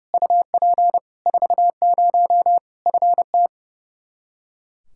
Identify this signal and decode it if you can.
Morse code, heard as a keyed tone, about 30 wpm, 700 Hz. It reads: UP 40 FT
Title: K1USN SST QSO Practice 30wpm